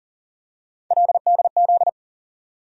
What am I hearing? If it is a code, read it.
Morse code, heard as a keyed tone, about 40 wpm, 700 Hz. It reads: LDZ
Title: USA Call Sign Postfixes - 3 Letters 40wpm